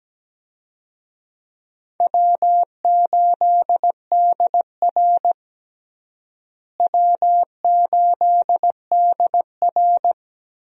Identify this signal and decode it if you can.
Morse code, heard as a keyed tone, about 17 wpm, 700 Hz. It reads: W8DR W8DR
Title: USA Call Signs - 4 Characters - Repeated 2x - 17wpm